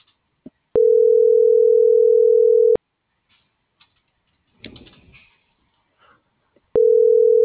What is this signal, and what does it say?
Ringback tone (North America), 440 + 480 Hz.